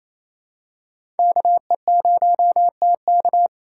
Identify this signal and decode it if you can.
Morse code, heard as a keyed tone, about 28 wpm, 700 Hz. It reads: KE0TK
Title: USA Call Signs - 5 Characters - 4x Character Spacing - 28wpm